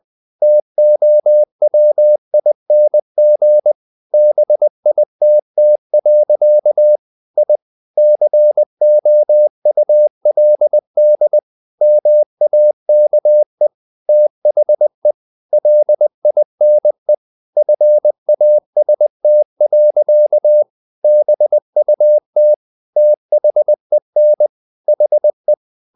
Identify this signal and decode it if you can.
Morse code, heard as a keyed tone, about 20 wpm, 600 Hz. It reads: TOWING BITT. I COULD MAKE THE LINE FAST. BUT THEN HE
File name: the old man and the sea / the old man and the sea part 1 20wpm